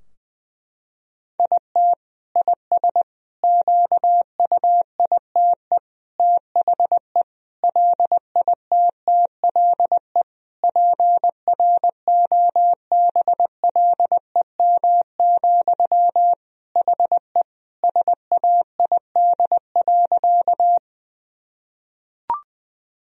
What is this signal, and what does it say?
Morse code, heard as a keyed tone, about 19 wpm, 700 Hz. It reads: IT IS QUITE THE LITTLE PROBLEM, HE SAID.
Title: Sentences from Top 1000 Words - Review Part 2 - 20wpm